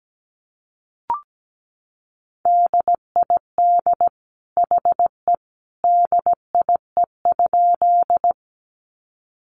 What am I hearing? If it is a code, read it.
Morse code, heard as a keyed tone, about 17 wpm, 700 Hz. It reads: DID HE DIE?
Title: Sentences from Top 1300 Words 17wpm